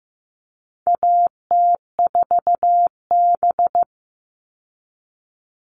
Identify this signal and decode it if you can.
Morse code, heard as a keyed tone, about 15 wpm, 700 Hz. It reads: AT4B
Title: All Cut Numbers - 4-Digit 15wpm